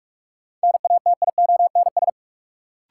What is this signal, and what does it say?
Morse code, heard as a keyed tone, about 46 wpm, 700 Hz. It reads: NATIONS
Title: Sets of 3 Words - New Word Drill - 45wpm